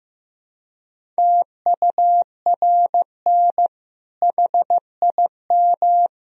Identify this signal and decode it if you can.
Morse code, heard as a keyed tone, about 15 wpm, 700 Hz. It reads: TURN HIM
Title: Sentences from Top 1000 Words - Review Part 1 - 15wpm